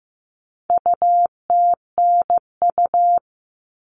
Morse code, heard as a keyed tone, about 15 wpm, 700 Hz. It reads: UTNU
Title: All Cut Numbers - 4-Digit 15wpm